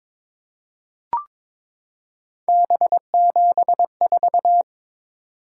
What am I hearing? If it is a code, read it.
Morse code, heard as a keyed tone, about 22 wpm, 700 Hz. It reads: B74
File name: practice / 22wpm / Three Letter-Number Combinations - 22wpm